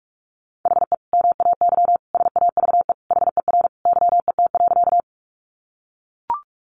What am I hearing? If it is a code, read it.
Morse code, heard as a keyed tone, about 44 wpm, 700 Hz. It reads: HE MAY SAVE HER YET.
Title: Sentences from Top 1100 Words 45wpm